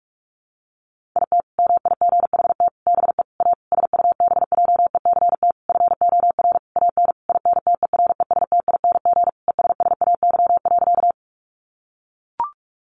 Morse code, heard as a keyed tone, about 44 wpm, 700 Hz. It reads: IT MIGHT BE A SUBJECT FOR AN INTERESTING ESSAY.
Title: Sentences from Top 1500 Words 45wpm